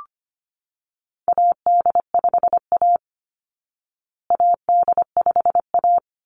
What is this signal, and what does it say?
Morse code, heard as a keyed tone, about 25 wpm, 700 Hz. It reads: AD5A AD5A
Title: Call Signs Repeated 3x - 25wpm